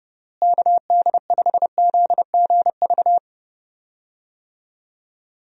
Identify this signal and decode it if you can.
Morse code, heard as a keyed tone, about 30 wpm, 700 Hz. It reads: KD5ZGV
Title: USA Call Signs - 6 Characters - Repeated 2x - 30wpm